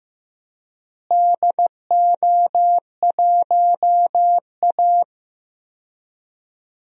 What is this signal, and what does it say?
Morse code, heard as a keyed tone, about 15 wpm, 700 Hz. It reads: DO1A
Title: DXCC-100 Call Signs - 4 Characters - Repeated 3x - 15wpm